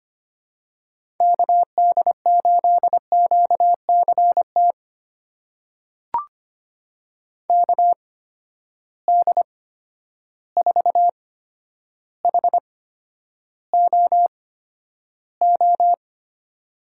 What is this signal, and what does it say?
Morse code, heard as a keyed tone, about 25 wpm, spaced as for 11 wpm, 700 Hz. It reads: KD8QCT KD4HOO
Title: USA Call Signs - 6 Characters - 8x Character Spacing - 25wpm